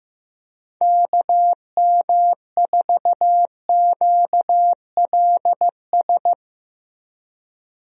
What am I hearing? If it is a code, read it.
Morse code, heard as a keyed tone, about 15 wpm, 700 Hz. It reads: KM4QLS
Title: USA Call Signs - 6 Characters - Repeated 2x - 15wpm